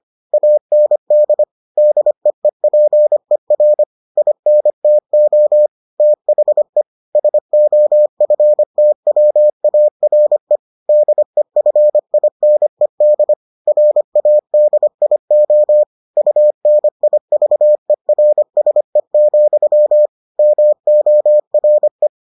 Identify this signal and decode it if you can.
Morse code, heard as a keyed tone, about 25 wpm, 600 Hz. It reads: AND DEEPER INTO THE SOFTWARE DEFINED RADIO UNIVERSE, MORE